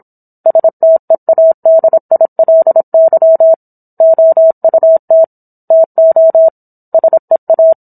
Morse code, heard as a keyed tone, about 27 wpm, 650 Hz. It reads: STEADILY OUT TO SEA